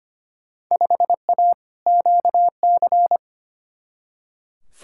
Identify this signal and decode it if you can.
Morse code, heard as a keyed tone, about 25 wpm, 700 Hz. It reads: 5A QC
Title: Field Day Exchanges Encore - 25wpm